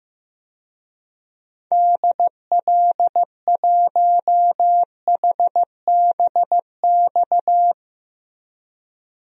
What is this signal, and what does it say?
Morse code, heard as a keyed tone, about 15 wpm, 700 Hz. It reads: DL1HBX
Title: DXCC-100 Call Signs - 6 Characters - Repeated 2x - 15wpm